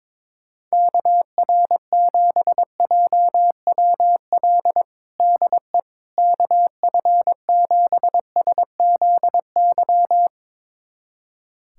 Morse code, heard as a keyed tone, about 22 wpm, 700 Hz. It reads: KR7JWL DE KF7SZY
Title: USA Callsign DE Callsign - 6 Characters - 22wpm